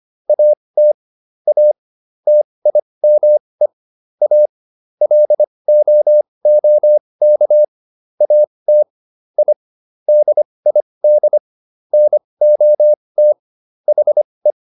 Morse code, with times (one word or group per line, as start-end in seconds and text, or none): 0.3-0.9 AT
1.5-1.7 A
2.3-3.7 TIME
4.2-4.5 A
5.0-7.6 LOOK
8.2-8.8 AT
9.4-9.5 I
10.1-11.4 DID
11.9-13.3 NOT
13.9-14.5 HE